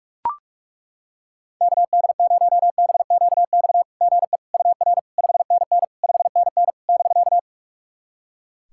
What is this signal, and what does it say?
Morse code, heard as a keyed tone, about 41 wpm, 690 Hz. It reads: KD0BQX GE UR 5NN 5NN <BK>
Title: SOTA QSO Practice 45wpm